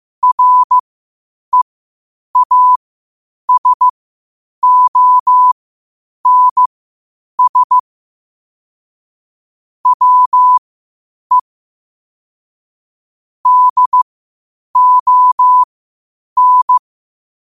Morse code, heard as a keyed tone, about 15 wpm, spaced as for 8 wpm, 1000 Hz. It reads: REASONS WE DON
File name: qotdfiles / 8wpm